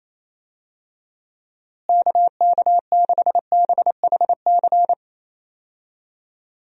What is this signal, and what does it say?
Morse code, heard as a keyed tone, about 28 wpm, 700 Hz. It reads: KK6BHC
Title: USA Call Signs - 6 Characters - Repeated 2x - 28wpm